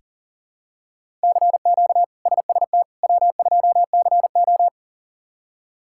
Morse code, heard as a keyed tone, about 39 wpm, 700 Hz. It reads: CQ SST W2CO
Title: K1USN SST QSO Practice - 40wpm